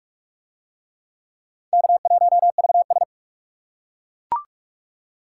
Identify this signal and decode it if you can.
Morse code, heard as a keyed tone, about 39 wpm, 700 Hz. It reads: K1VS